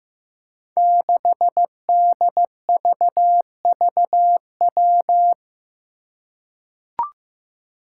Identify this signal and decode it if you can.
Morse code, heard as a keyed tone, about 15 wpm, 700 Hz. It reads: 6DVVW
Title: Five Challenging Letter-Number Combinations 15wpm